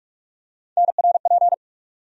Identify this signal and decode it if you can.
Morse code, heard as a keyed tone, about 46 wpm, 700 Hz. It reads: NRP
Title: USA Call Sign Postfixes - 3 Letters 45wpm